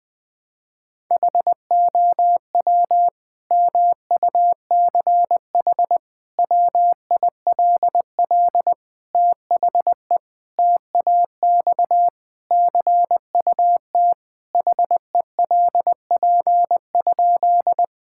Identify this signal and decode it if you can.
Morse code, heard as a keyed tone, about 21 wpm, 700 Hz. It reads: HOW MUCH WILL THE TAX CUT HELP?
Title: Sentences from Top 1000 Words - Review Part 5 - 20wpm